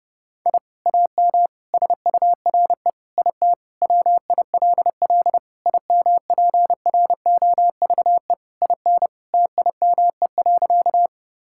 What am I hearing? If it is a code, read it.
Morse code, heard as a keyed tone, about 32 wpm, 700 Hz. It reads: I AM SURE IT WILL IMPROVE IN TIME.
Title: Sentences from Top 1000 Words - Review Part 5 - 30wpm